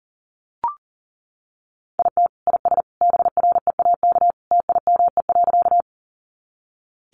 Morse code, heard as a keyed tone, about 40 wpm, 700 Hz. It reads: IT IS BREAK TIME.
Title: Easy Phrases 40wpm